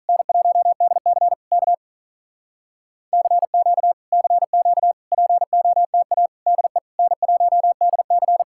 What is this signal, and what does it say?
Morse code, heard as a keyed tone, about 46 wpm, 700 Hz. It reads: N1DC K CQ CQ POTA DE N1DC